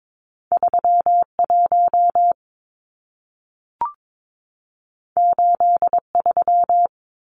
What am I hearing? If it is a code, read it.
Morse code, heard as a keyed tone, about 22 wpm, 700 Hz. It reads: 31 83
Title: Numbers - 2-Digit 22wpm